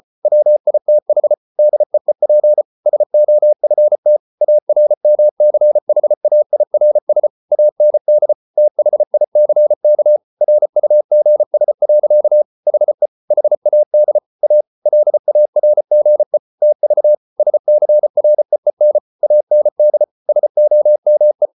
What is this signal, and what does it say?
Morse code, heard as a keyed tone, about 34 wpm, 600 Hz. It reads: WITH DEEP SOFT ARMCHAIRS AND THICK RUGS. HE HAD A LARGE TV SCREEN AND SOME